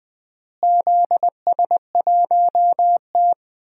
Morse code, heard as a keyed tone, about 20 wpm, 700 Hz. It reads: ZS1T
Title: DXCC-100 Call Signs - 4 Characters - Repeated 3x - 20wpm